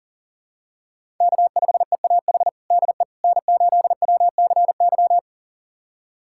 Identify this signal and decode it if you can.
Morse code, heard as a keyed tone, about 37 wpm, 700 Hz. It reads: K5EAH DE N8WCY
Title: USA Callsign DE Callsign - 5 Characters - 40wpm